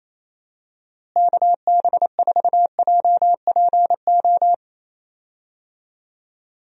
Morse code, heard as a keyed tone, about 28 wpm, 700 Hz. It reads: KB4JPO
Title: USA Call Signs - 6 Characters - Repeated 3x - 28wpm